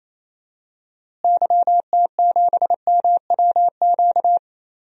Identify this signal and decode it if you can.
Morse code, heard as a keyed tone, about 28 wpm, 700 Hz. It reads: YT7MWQ
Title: DXCC-100 Call Signs - 6 Characters - Repeated 3x - 28wpm